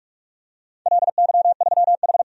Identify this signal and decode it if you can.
Morse code, heard as a keyed tone, about 47 wpm, 700 Hz. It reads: LY3H
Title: DXCC-100 Call Signs - 4 Characters - 45wpm